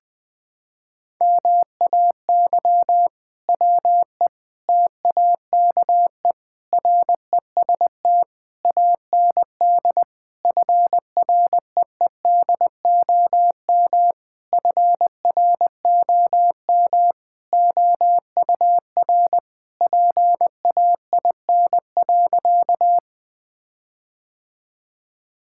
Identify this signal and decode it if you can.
Morse code, heard as a keyed tone, about 19 wpm, 700 Hz. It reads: MAY WE TAKE REST AND FREEDOM FROM OUR PAIN.
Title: Sentences from Top 1600 Words 20wpm